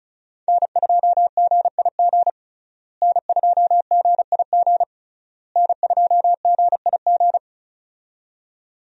Morse code, heard as a keyed tone, about 36 wpm, 700 Hz. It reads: N2GIG N2GIG N2GIG
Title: USA Call Signs - 5 Characters - Repeated 3x - 35wpm